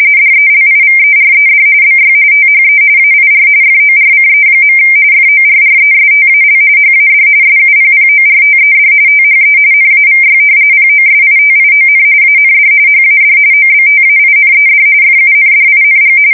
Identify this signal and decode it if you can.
RTTY, about 45.45 baud. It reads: DELIVERED BY THE AIR FORCE. FORRESTAL HAD SUPPORTED THE NAVY POSITION AND HAD OBTAINED FUNDING
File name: fsk170 / usnavy5